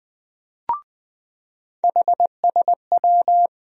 Morse code, heard as a keyed tone, about 20 wpm, 700 Hz. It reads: HSW
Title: USA Call Sign Postfixes - 3 Letters 20wpm